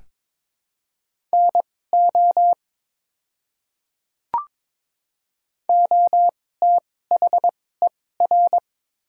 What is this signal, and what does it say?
Morse code, heard as a keyed tone, about 21 wpm, spaced as for 16 wpm, 700 Hz. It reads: NO OTHER
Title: 2x Character Spacing - Top 100 Words - 22wpm